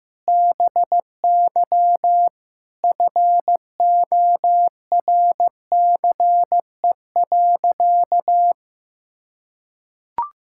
Morse code, heard as a keyed tone, about 16 wpm, 700 Hz. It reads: BY FORCE.
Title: Sentences from Top 1000 Words - Review Part 6 - 15wpm